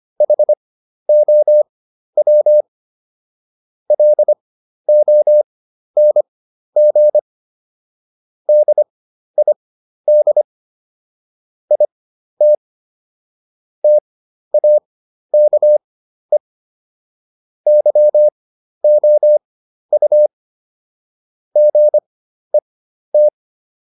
Morse code, heard as a keyed tone, about 25 wpm, spaced as for 12 wpm, 600 Hz. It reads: HOW LONG DID IT TAKE YOU GET